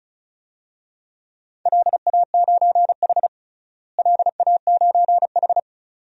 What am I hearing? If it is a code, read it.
Morse code, heard as a keyed tone, about 36 wpm, 700 Hz. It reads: LA9H LA9H
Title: DXCC-100 Call Signs - 4 Characters - Repeated 2x - 35wpm